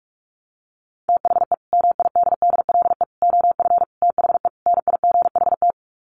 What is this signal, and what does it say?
Morse code, heard as a keyed tone, about 44 wpm, 700 Hz. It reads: THE MIDDLE OF THE NIGHT
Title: Sets of 5 Words 45wpm